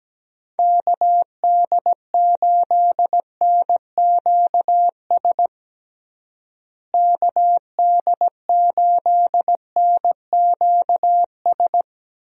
Morse code, heard as a keyed tone, about 17 wpm, 700 Hz. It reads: KD8NQS KD8NQS
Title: USA Call Signs - 6 Characters - Repeated 3x - 17wpm